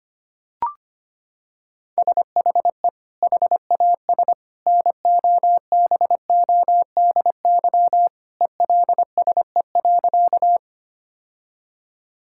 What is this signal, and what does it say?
Morse code, heard as a keyed tone, about 25 wpm, 700 Hz. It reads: SHE HAS NOBODY ELSE.
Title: Sentences from Top 1600 Words 25wpm